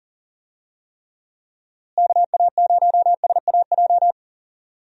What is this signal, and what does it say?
Morse code, heard as a keyed tone, about 40 wpm, 700 Hz. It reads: KA0SAJ
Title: USA Call Signs - 6 Characters - Repeated 3x - 40wpm